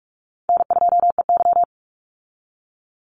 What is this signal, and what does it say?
Morse code, heard as a keyed tone, about 44 wpm, 700 Hz. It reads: N2EY
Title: Easy Call Signs Repeated 3x - 45wpm